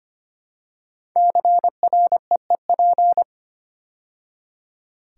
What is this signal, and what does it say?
Morse code, heard as a keyed tone, about 25 wpm, 700 Hz. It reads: CREEP
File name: 5 Letter Words - First 3 Letters Given - 25wpm